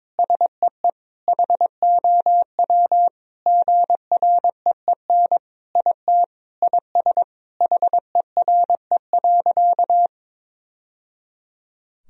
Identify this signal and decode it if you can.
Morse code, heard as a keyed tone, about 22 wpm, 700 Hz. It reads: SEE HOW GREEN IT IS HERE.
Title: Sentences from Top 1000 Words - Review Part 2 - 22wpm